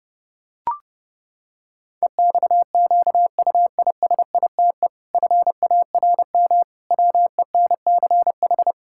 Morse code, heard as a keyed tone, about 30 wpm, 700 Hz. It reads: EXQUISITE FARM WENCH
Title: Pangram Challenge v2 - 30wpm